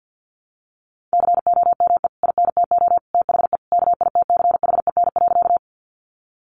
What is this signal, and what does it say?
Morse code, heard as a keyed tone, about 50 wpm, 700 Hz. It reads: COME INTO THE KITCHEN.
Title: Sentences from Top 1400 Words 50wpm